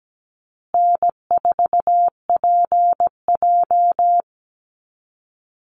Morse code, heard as a keyed tone, about 17 wpm, 700 Hz. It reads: N4PJ
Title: Easy Call Signs Repeated 3x - 17wpm